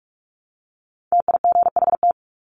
Morse code, heard as a keyed tone, about 44 wpm, 700 Hz. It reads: TIGHT
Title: Top 1751-2000 Words 45wpm